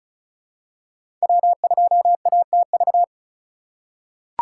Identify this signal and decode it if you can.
Morse code, heard as a keyed tone, about 36 wpm, 700 Hz. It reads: W2ATV
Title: USA Call Signs - 5 Characters - 8x Character Spacing - 35wpm